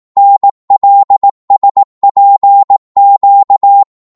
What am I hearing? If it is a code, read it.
Morse code, heard as a keyed tone, about 18 wpm, 800 Hz. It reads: NLSPQ